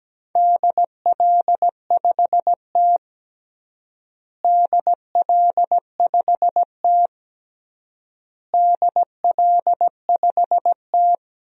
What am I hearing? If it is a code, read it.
Morse code, heard as a keyed tone, about 17 wpm, 700 Hz. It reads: DL5T DL5T DL5T
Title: DXCC-100 Call Signs - 4 Characters - Repeated 3x - 17wpm